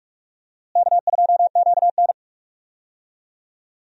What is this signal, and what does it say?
Morse code, heard as a keyed tone, about 46 wpm, 700 Hz. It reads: K2QN
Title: USA Call Signs - 4 Characters - Repeated 2x - 45wpm